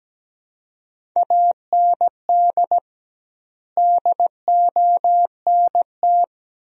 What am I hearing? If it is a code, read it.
Morse code, heard as a keyed tone, about 17 wpm, 700 Hz. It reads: AND DONT
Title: 2x Word-Spacing - Sentences from Top 400 Words - 17wpm